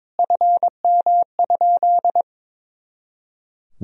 Morse code, heard as a keyed tone, about 22 wpm, 700 Hz. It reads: FM?
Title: Sentences from Top 200 Words and Abbreviations 22wpm